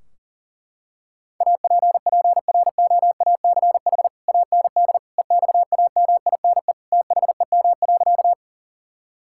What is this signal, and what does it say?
Morse code, heard as a keyed tone, about 40 wpm, 700 Hz. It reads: APPROACH AND EXAMINE THEM.
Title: Sentences from Top 1600 Words 40wpm